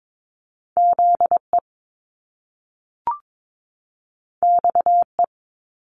Morse code, heard as a keyed tone, about 22 wpm, 700 Hz. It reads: Z E X E
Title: Top Two Letter Combinations in Words 22wpm